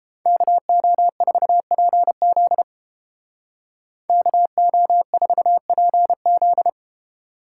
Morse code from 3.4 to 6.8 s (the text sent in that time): KO4PZ